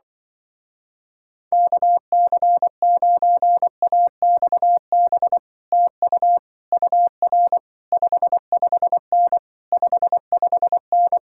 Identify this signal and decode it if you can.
Morse code, heard as a keyed tone, about 23 wpm, 700 Hz. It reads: KC9AXB TU UR 55N 55N
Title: Speed-Racing Common QSO Phrases - 20wpm